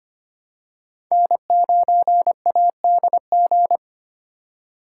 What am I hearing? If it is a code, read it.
Morse code, heard as a keyed tone, about 25 wpm, 700 Hz. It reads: N9ADG
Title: K1USN SST QSO Practice - 25wpm